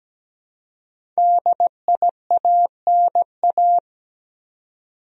Morse code, heard as a keyed tone, about 17 wpm, 700 Hz. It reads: DIANA
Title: USA Common Names - Max 5 Letters - 17wpm